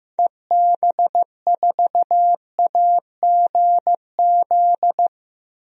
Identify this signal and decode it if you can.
Morse code, heard as a keyed tone, about 15 wpm, 700 Hz. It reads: EB4AGZ
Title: DXCC-100 Call Signs - 6 Characters - Repeated 3x - 15wpm